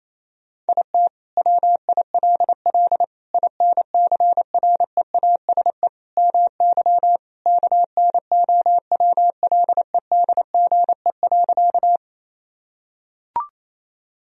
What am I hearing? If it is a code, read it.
Morse code, heard as a keyed tone, about 28 wpm, 700 Hz. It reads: IT WILL INCREASE MY KNOWLEDGE.
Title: Sentences from Top 1000 Words - Review Part 4 - 28wpm